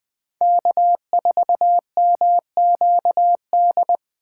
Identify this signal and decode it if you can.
Morse code, heard as a keyed tone, about 20 wpm, 700 Hz. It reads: K4MQD
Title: Speed Racing - USA 1x3 Call Signs 17wpm